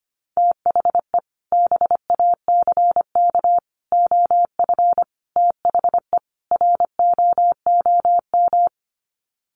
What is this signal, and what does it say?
Morse code, heard as a keyed tone, about 25 wpm, 700 Hz. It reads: THE BACK OF THE ROOM
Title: Sets of 5 Words Encore 25wpm